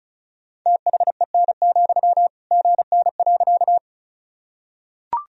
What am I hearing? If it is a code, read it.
Morse code, heard as a keyed tone, about 36 wpm, 700 Hz. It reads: THEN, GN.
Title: Sentences from Top 300 Words and Abbreviations 35wpm